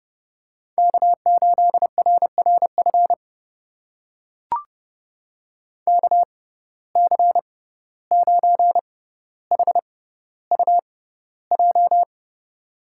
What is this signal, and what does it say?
Morse code, heard as a keyed tone, about 32 wpm, spaced as for 15 wpm, 700 Hz. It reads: K8RRF KC9HUJ
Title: USA Call Signs - 6x Character Spacing - 30wpm